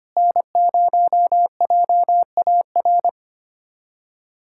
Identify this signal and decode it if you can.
Morse code, heard as a keyed tone, about 25 wpm, 700 Hz. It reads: N0JAR
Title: USA Callsign DE Callsign - 5 Characters - 25wpm